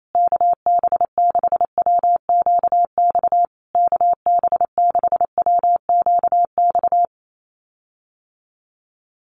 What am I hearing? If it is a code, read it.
Morse code, heard as a keyed tone, about 28 wpm, 690 Hz. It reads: KB6WQX KB6WQX
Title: USA Call Signs Repeated 3x 28wpm